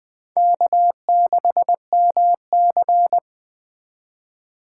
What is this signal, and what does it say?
Morse code, heard as a keyed tone, about 19 wpm, 700 Hz. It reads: K6MC